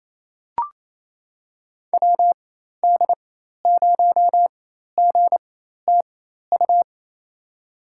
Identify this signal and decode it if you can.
Morse code, heard as a keyed tone, about 28 wpm, spaced as for 13 wpm, 700 Hz. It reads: WD0GTU
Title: USA Call Signs - 6 Characters - 4x Character Spacing - 28wpm